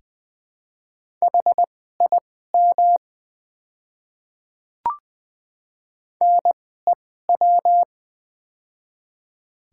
Morse code, heard as a keyed tone, about 20 wpm, spaced as for 14 wpm, 700 Hz. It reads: HIM NEW
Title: 2x Character Spacing - Top 200 Words - 20wpm